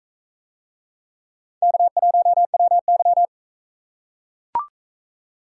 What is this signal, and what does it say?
Morse code, heard as a keyed tone, about 44 wpm, 700 Hz. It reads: K1WY